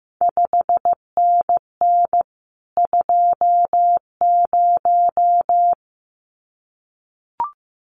Morse code, heard as a keyed tone, about 15 wpm, 700 Hz. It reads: 5NN 20
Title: CQ WW DX Contest Exchange Encore - 15wpm